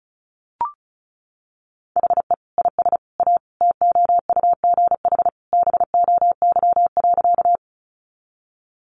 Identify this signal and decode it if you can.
Morse code, heard as a keyed tone, about 34 wpm, 700 Hz. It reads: HE IS A TOUGH BOY.
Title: Sentences from Top 1200 Words 35wpm